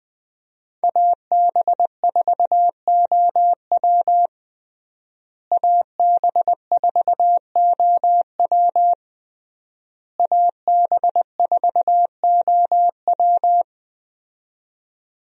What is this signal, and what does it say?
Morse code, heard as a keyed tone, about 20 wpm, 700 Hz. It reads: AB4OW AB4OW AB4OW
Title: USA Call Signs - 5 Characters - Repeated 3x - 20wpm